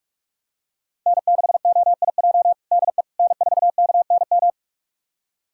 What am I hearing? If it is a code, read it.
Morse code, heard as a keyed tone, about 46 wpm, 700 Hz. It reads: N6OIJ DE N4KNM
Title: USA Callsign DE Callsign - 5 Characters - 45wpm